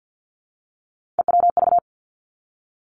Morse code, heard as a keyed tone, about 50 wpm, 700 Hz. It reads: EWV
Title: Three Challenging Letter-Number Combinations 50wpm